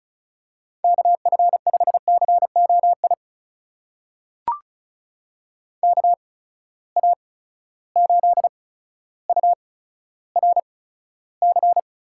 Morse code, heard as a keyed tone, about 35 wpm, spaced as for 15 wpm, 700 Hz. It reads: KF5COI KA8URC
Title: USA Call Signs - 6 Characters - 8x Character Spacing - 35wpm